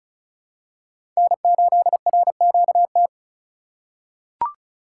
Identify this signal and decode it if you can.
Morse code, heard as a keyed tone, about 34 wpm, 700 Hz. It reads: N8RQT
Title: USA Call Signs - 5 Characters - 2x Character Spacing - 35wpm